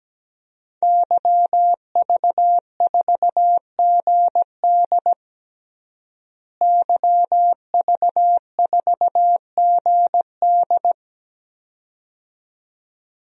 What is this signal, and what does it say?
Morse code, heard as a keyed tone, about 17 wpm, 700 Hz. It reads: YV4GD YV4GD
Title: DXCC-100 Call Signs - 5 Characters - Repeated 2x - 17wpm